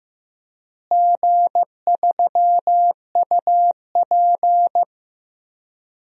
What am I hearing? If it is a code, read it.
Morse code, heard as a keyed tone, about 15 wpm, 700 Hz. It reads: G3UP
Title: DXCC-100 Call Signs - 4 Characters - Repeated 2x - 15wpm